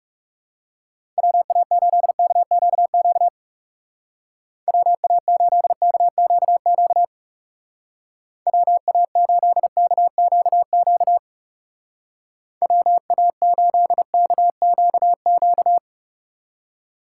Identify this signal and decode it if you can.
Morse code, heard as a keyed tone, about 36 wpm, 700 Hz. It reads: WA8KQQ WA8KQQ WA8KQQ WA8KQQ
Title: Speed Racing - USA Call Signs 30wpm